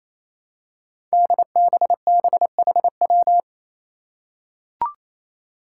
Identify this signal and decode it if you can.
Morse code, heard as a keyed tone, about 28 wpm, 700 Hz. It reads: DBBHW
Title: Five Challenging Letter-Number Combinations 28wpm